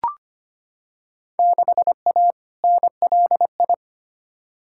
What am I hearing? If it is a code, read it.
Morse code, heard as a keyed tone, about 25 wpm, 700 Hz. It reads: 6A NLI
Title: Field Day Exchanges Encore - 25wpm